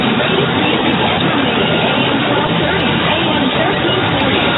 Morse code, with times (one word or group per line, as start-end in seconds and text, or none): none